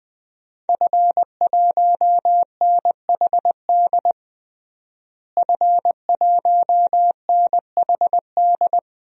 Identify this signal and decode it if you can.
Morse code, heard as a keyed tone, about 21 wpm, 700 Hz. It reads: F1NHD F1NHD
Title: DXCC-100 Call Signs - 5 Characters - Repeated 2x - 20wpm